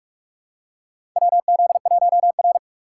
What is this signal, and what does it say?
Morse code, heard as a keyed tone, about 45 wpm, 700 Hz. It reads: WZ1R
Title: USA Call Signs - 4 Characters - 3x Character Spacing - 45wpm